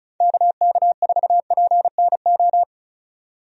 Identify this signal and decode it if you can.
Morse code, heard as a keyed tone, about 36 wpm, 700 Hz. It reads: KK4PNO
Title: USA Call Signs - 6 Characters - 35wpm